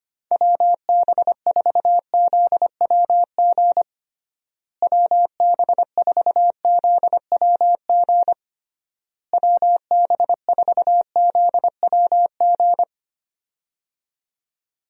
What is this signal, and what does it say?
Morse code, heard as a keyed tone, about 25 wpm, 700 Hz. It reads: WB4ZWG WB4ZWG WB4ZWG
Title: USA Call Signs - 6 Characters - Repeated 3x - 25wpm